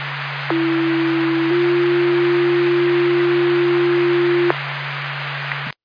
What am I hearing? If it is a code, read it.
Two-tone page, heard as a one-second tone, then a three-second tone, about 330 then 350 Hz, with noise. It is heard from 0.5 to 4.5 s.